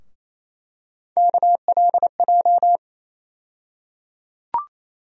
Morse code, heard as a keyed tone, about 28 wpm, 700 Hz. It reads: KLJ
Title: USA Call Sign Postfixes - 3 Letters 28wpm